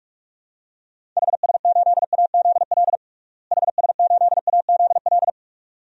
Morse code, heard as a keyed tone, about 46 wpm, 700 Hz. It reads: HS8AZL HS8AZL
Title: DXCC-100 Call Signs - 6 Characters - Repeated 2x - 45wpm